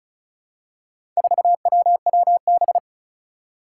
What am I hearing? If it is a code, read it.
Morse code, heard as a keyed tone, about 36 wpm, 700 Hz. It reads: 4WWB
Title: Four Challenging Letter-Number Combinations 35wpm